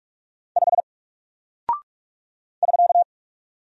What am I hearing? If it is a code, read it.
Morse code, heard as a keyed tone, about 44 wpm, 700 Hz. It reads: 5 <SK>
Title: Single Character (Letter:Number:Punctuation:Prosign) Alt 45wpm